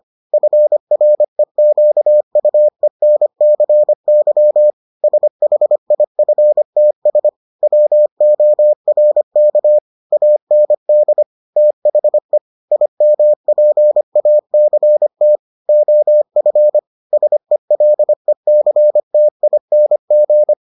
Morse code, heard as a keyed tone, about 25 wpm, 600 Hz. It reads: FREQUENCY SHIFTS WORK AND THE IMPACT OF SELECTING